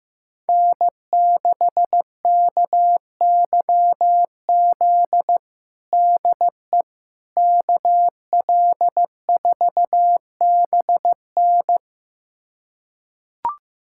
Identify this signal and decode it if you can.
Morse code, heard as a keyed tone, about 15 wpm, 700 Hz. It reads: N6KYZ DE KL4BN
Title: USA Callsign DE Callsign - 5 Characters - 15wpm